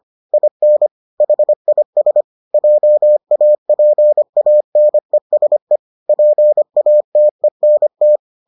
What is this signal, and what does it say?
Morse code, heard as a keyed tone, about 25 wpm, 600 Hz. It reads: IN HIS JAPANESE PATENT